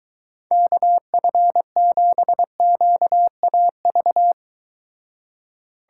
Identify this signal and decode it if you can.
Morse code, heard as a keyed tone, about 23 wpm, 700 Hz. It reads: KF7QAV